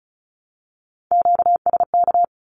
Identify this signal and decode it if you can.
Morse code, heard as a keyed tone, about 34 wpm, 700 Hz. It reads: QSK
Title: Q-Codes Repeated 3x 35wpm